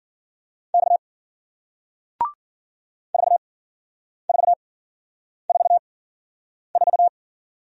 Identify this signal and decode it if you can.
Morse code, heard as a keyed tone, about 53 wpm, spaced as for 9 wpm, 700 Hz. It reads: X 4444